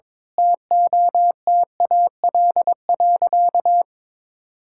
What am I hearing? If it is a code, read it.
Morse code, heard as a keyed tone, about 22 wpm, 700 Hz. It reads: TOTAL.
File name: practice / 22wpm / Sentences with 8-Word Phrases - 22wpm